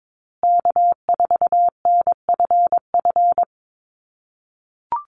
Morse code, heard as a keyed tone, about 22 wpm, 700 Hz. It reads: K4NFF
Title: USA Call Signs Repeated 2x 22wpm